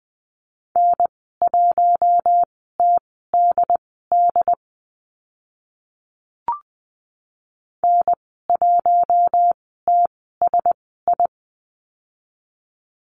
Morse code, heard as a keyed tone, about 20 wpm, spaced as for 14 wpm, 700 Hz. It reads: N1TDD N1TSI
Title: 076 Lesson 13g - Letter L - Call Signs 20wpm